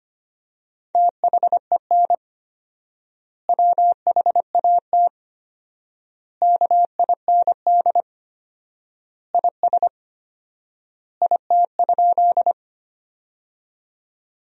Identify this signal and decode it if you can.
Morse code, heard as a keyed tone, about 25 wpm, 700 Hz. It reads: THEN WHAT KIND IS IT?
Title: 4x Word-Spacing - Sentences from Top 200 Words - 25wpm